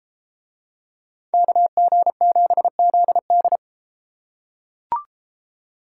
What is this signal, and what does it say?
Morse code, heard as a keyed tone, about 32 wpm, 700 Hz. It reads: KG7ZD